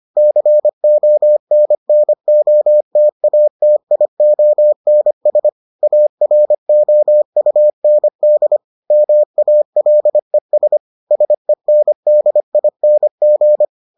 Morse code, heard as a keyed tone, about 25 wpm, 600 Hz. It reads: CONNOTATIONS AROUND MALES SENDING